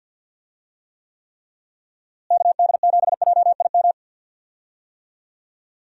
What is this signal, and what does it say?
Morse code, heard as a keyed tone, about 48 wpm, 700 Hz. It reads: KD7JIM
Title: USA Call Signs - 6 Characters - Repeated 3x - 50wpm